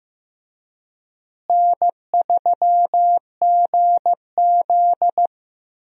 Morse code, heard as a keyed tone, about 15 wpm, 700 Hz. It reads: N3GZ
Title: USA Call Signs - 4 Characters - Repeated 3x - 15wpm